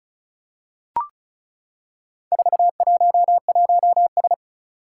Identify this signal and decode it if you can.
Morse code, heard as a keyed tone, about 36 wpm, 700 Hz. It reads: 411S
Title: Four Challenging Letter-Number Combinations 35wpm